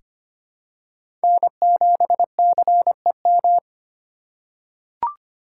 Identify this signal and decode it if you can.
Morse code, heard as a keyed tone, about 25 wpm, 700 Hz. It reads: N7CEM
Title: USA Call Signs - 5 Characters - 2x Character Spacing - 25wpm